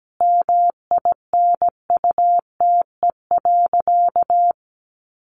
Morse code, heard as a keyed tone, about 17 wpm, 700 Hz. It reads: MINUTE.
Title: Sentences from Top 500 Words 17wpm